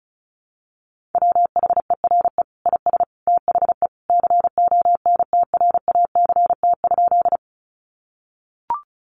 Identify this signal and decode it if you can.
Morse code, heard as a keyed tone, about 34 wpm, 700 Hz. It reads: WHERE IS THE CONTRACT?
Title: Sentences from Top 1400 Words 35wpm